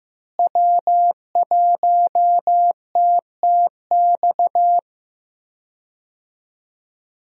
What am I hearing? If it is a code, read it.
Morse code, heard as a keyed tone, about 15 wpm, 700 Hz. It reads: W1TTX
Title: USA Call Signs - 5 Characters - Repeated 2x - 15wpm